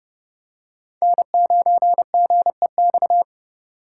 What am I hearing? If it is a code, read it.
Morse code, heard as a keyed tone, about 29 wpm, 700 Hz. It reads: N9GEX